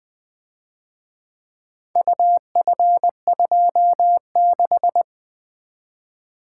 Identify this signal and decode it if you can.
Morse code, heard as a keyed tone, about 23 wpm, 700 Hz. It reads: UF26
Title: Speed-Racing - Four Letter-Number Combinations - 20wpm